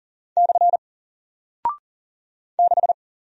Morse code, heard as a keyed tone, about 40 wpm, 700 Hz. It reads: / 6
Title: Single Character (Letter:Number:Punctuation:Prosign) Alt 40wpm